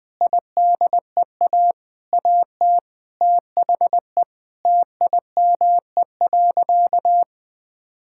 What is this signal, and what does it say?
Morse code, heard as a keyed tone, about 20 wpm, 700 Hz. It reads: IDEA AT THE TIME.
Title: Sentences with 8-Word Phrases - 20wpm